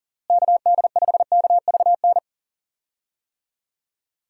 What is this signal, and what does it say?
Morse code, heard as a keyed tone, about 40 wpm, 700 Hz. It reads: KD5KVN
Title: USA Call Signs - 6 Characters - Repeated 3x - 40wpm